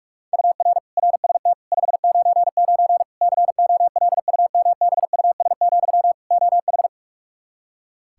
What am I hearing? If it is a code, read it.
Morse code, heard as a keyed tone, about 47 wpm, 700 Hz. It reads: UR RST 599 COLUMBUS, OH
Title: K1USN SST QSO Practice 45wpm